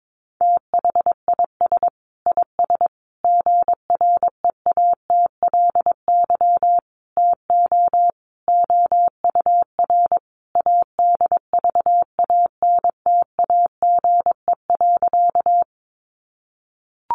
Morse code, heard as a keyed tone, about 22 wpm, 700 Hz. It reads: THIS IS GREATLY TO OUR ADVANTAGE.
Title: Sentences from Top 1400 Words 22wpm